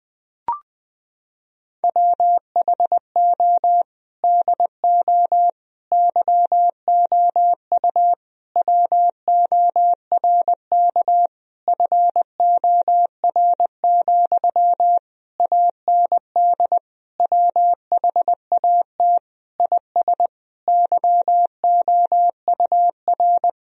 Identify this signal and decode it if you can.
Morse code, heard as a keyed tone, about 20 wpm, 700 Hz. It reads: WHO DO YOU WORK FOR, AND WHAT IS YOUR
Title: Sentences from Top 1000 Words - Review Part 4 - 20wpm